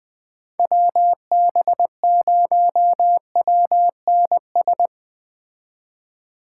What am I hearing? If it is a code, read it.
Morse code, heard as a keyed tone, about 19 wpm, 700 Hz. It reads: WB0WNS
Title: Speed Racing - USA 2x3 Call Signs 17wpm